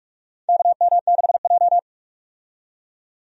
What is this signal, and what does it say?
Morse code, heard as a keyed tone, about 41 wpm, 700 Hz. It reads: KM6J
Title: USA Call Signs - 4 Characters - 45wpm